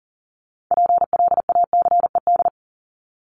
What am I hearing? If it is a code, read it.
Morse code, heard as a keyed tone, about 40 wpm, 700 Hz. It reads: PLACED
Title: ED Suffix - Top 100 Words with Base Words 40wpm